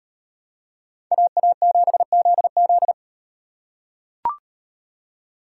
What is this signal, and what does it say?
Morse code, heard as a keyed tone, about 38 wpm, 700 Hz. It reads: AA7ZZ